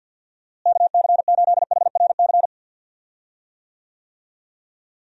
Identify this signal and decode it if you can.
Morse code, heard as a keyed tone, about 50 wpm, 700 Hz. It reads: KC8HRC
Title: USA Call Signs - 6 Characters - Repeated 3x - 50wpm